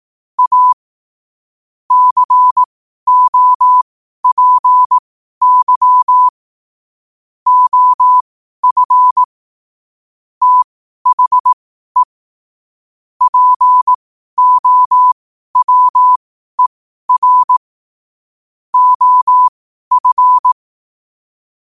Morse code, heard as a keyed tone, about 18 wpm, spaced as for 12 wpm, 1000 Hz. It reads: A COPY OF THE POWER OF
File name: qotdfiles / 12wpm